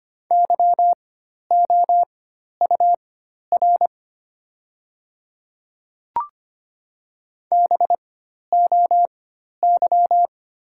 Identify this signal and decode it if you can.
Morse code, heard as a keyed tone, about 25 wpm, spaced as for 12 wpm, 700 Hz. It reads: YOUR BOY
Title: 4x Character Spacing - Top 300 Words - 25wpm